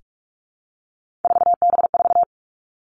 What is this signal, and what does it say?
Morse code, heard as a keyed tone, about 46 wpm, 700 Hz. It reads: 4B4
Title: Three Challenging Letter-Number Combinations 45wpm